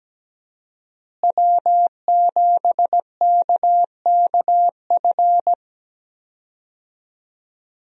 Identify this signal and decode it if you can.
Morse code, heard as a keyed tone, about 17 wpm, 700 Hz. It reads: W7KKF
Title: USA Call Signs - 5 Characters - Repeated 2x - 17wpm